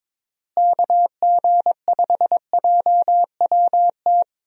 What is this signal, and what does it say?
Morse code, heard as a keyed tone, about 22 wpm, 700 Hz. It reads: KG5JWT
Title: USA Call Signs - 6 Characters - Repeated 3x - 22wpm